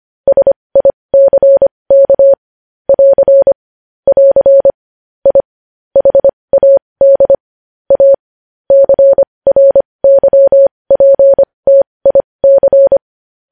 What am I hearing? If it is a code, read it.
Morse code, heard as a keyed tone, about 25 wpm, spaced as for 20 wpm, 550 Hz. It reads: SICK <AR> <AR> I HAD A CRYPTIC
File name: podcast / News headlines-1x-25x20